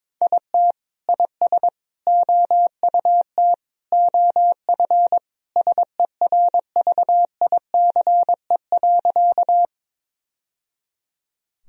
Morse code, heard as a keyed tone, about 22 wpm, 700 Hz. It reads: IT IS OUT OF SERVICE.
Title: Sentences from Top 1000 Words - Review Part 4 - 22wpm